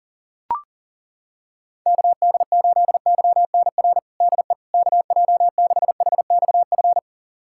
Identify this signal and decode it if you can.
Morse code, heard as a keyed tone, about 40 wpm, 700 Hz. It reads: KD8YNR DE KJ6HXF
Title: USA Callsign DE Callsign 40wpm